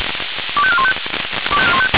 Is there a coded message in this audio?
Yes: five-tone selective call (ZVEI) from 0.6 to 0.9 s, heard as quick runs of five tones, tones of about 70 ms. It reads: 25415